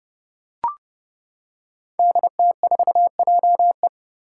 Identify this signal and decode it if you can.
Morse code, heard as a keyed tone, about 30 wpm, 700 Hz. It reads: DT4JE
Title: Five Challenging Letter-Number Combinations 30wpm